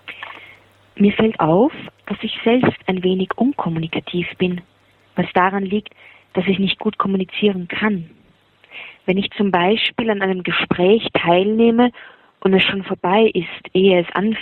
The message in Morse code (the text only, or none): none